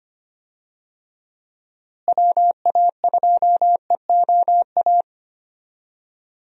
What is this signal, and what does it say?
Morse code, heard as a keyed tone, about 25 wpm, 700 Hz. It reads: WA2EOA
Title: USA Call Signs - 6 Characters - Repeated 3x - 25wpm